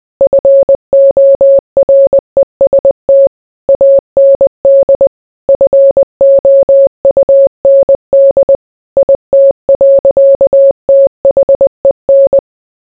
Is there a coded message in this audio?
Yes: Morse code from 0.2 to 12.4 s, heard as a keyed tone, about 20 wpm, 550 Hz. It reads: FOREST AND FOUND IT. THEN